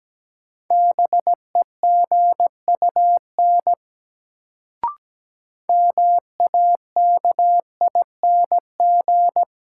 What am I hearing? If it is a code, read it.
Morse code, heard as a keyed tone, about 17 wpm, 700 Hz. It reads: BEGUN MAKING
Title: Sentences from Top 1000 Words - Review Part 6 - New Word Drill - 17wpm